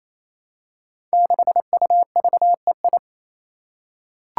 Morse code, heard as a keyed tone, about 28 wpm, 700 Hz. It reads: 6UVEI
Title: Five Challenging Letter-Number Combinations 28wpm